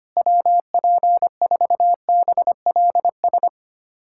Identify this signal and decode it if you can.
Morse code, heard as a keyed tone, about 25 wpm, 700 Hz. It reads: WP4BLS
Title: USA Callsign DE Callsign - 6 Characters - 25wpm